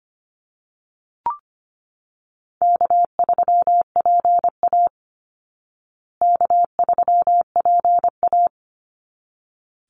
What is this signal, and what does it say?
Morse code, heard as a keyed tone, about 25 wpm, 700 Hz. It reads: K3PA K3PA
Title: Easy Call Signs Repeated 2x - 25wpm